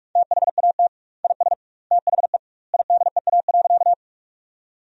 Morse code, heard as a keyed tone, about 43 wpm, 700 Hz. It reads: THAT IS THE IDEA.
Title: 2x Word-Spacing - Sentences from Top 300 Words - 45wpm